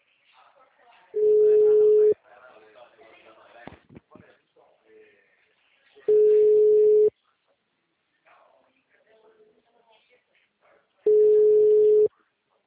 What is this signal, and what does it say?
Ringback tone (Europe, 425 Hz), with speech.
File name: out- -2013- - - .2143